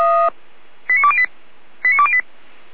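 Five-tone selective call (ZVEI), heard as quick runs of five tones, tones of about 70 ms. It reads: 89208 89208
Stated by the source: ZVEI five-tone call 89508 received on SDR, from the Feuerwehr Orlamünde (fire brigade) audio server.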